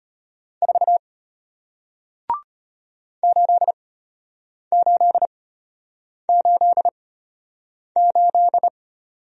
Morse code, heard as a keyed tone, about 33 wpm, spaced as for 8 wpm, 700 Hz. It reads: 4 8888